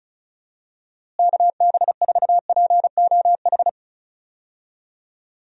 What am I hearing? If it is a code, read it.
Morse code, heard as a keyed tone, about 36 wpm, 700 Hz. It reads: KB4POH
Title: USA Call Signs - 6 Characters - Repeated 3x - 35wpm